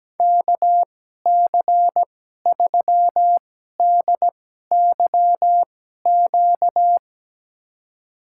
Morse code, heard as a keyed tone, about 17 wpm, spaced as for 12 wpm, 700 Hz. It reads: KC3DYQ
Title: USA Call Signs - 6 Characters - 2x Character Spacing - 17wpm